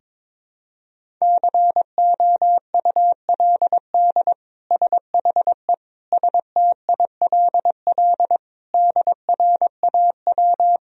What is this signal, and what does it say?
Morse code, heard as a keyed tone, about 22 wpm, 700 Hz. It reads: COULD SHE STILL DRAW
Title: Sentences from Top 1000 Words - Review Part 1 - 22wpm